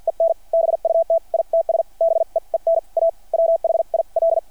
Morse code, heard as a keyed tone, about 50 wpm, spaced as for 42 wpm, 660 Hz. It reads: EN BUT ITS BEEN A WHIL
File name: archived / CRSnet